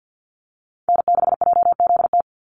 Morse code, heard as a keyed tone, about 50 wpm, 700 Hz. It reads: N6JZT
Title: NA SOTA Activator Callsigns 50wpm